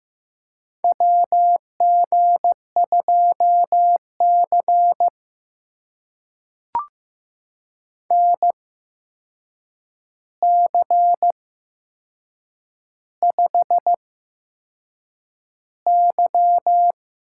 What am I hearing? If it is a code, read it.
Morse code, heard as a keyed tone, about 15 wpm, spaced as for 6 wpm, 700 Hz. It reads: WG2C NC5Y
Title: USA Call Signs - 4 Characters - 8x Character Spacing - 15wpm